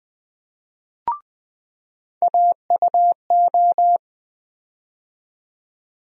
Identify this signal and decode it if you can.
Morse code, heard as a keyed tone, about 20 wpm, 700 Hz. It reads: AUO
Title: USA Call Sign Postfixes - 3 Letters 20wpm